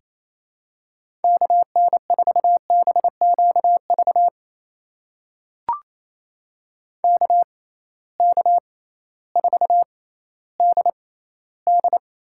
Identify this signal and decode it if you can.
Morse code, heard as a keyed tone, about 28 wpm, spaced as for 15 wpm, 700 Hz. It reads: KN4BQV KK4DD
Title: USA Call Signs - 6x Character Spacing - 28wpm